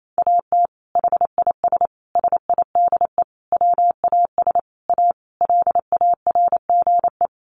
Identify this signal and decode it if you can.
Morse code, heard as a keyed tone, about 28 wpm, 700 Hz. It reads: AT HIS SIDE WAS A LARGE
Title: Sentences from Top 500 Words 28wpm